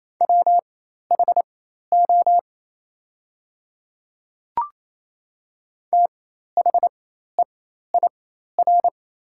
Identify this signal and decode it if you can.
Morse code, heard as a keyed tone, about 28 wpm, spaced as for 13 wpm, 700 Hz. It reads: WHO THEIR
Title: 4x Character Spacing - Top 100 Words - 28wpm